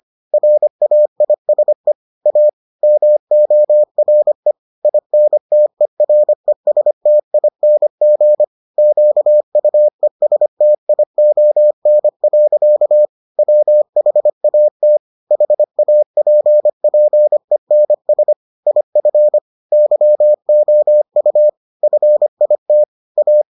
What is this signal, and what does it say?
Morse code, heard as a keyed tone, about 25 wpm, 600 Hz. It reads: RAISE A MORE INTERESTING QUESTION. WHAT HAPPENS IF YOU FIT A